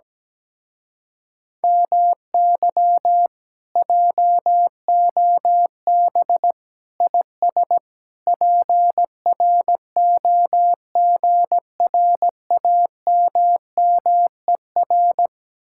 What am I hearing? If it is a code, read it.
Morse code, heard as a keyed tone, about 17 wpm, 700 Hz. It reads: MY JOB IS PROGRAMMER
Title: Speed-Racing Common QSO Phrases Encore - 17wpm